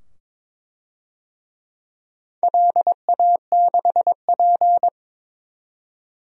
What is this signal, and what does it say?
Morse code, heard as a keyed tone, about 23 wpm, 700 Hz. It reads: LA6P
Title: DXCC-100 Call Signs - 4 Characters - 22wpm